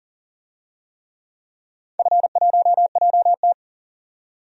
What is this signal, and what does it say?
Morse code, heard as a keyed tone, about 40 wpm, 700 Hz. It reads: F1JT
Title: DXCC-100 Call Signs - 4 Characters - 40wpm